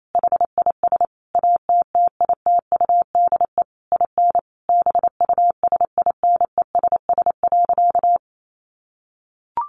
Morse code, heard as a keyed tone, about 28 wpm, 700 Hz. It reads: HIS ATTITUDE IN BUSINESS.
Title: Sentences from Top 1300 Words 28wpm